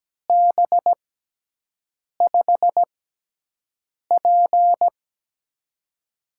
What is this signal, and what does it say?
Morse code, heard as a keyed tone, about 17 wpm, spaced as for 6 wpm, 700 Hz. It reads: B5P
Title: USA Call Signs - 4 Characters - 6x Character Spacing - 17wpm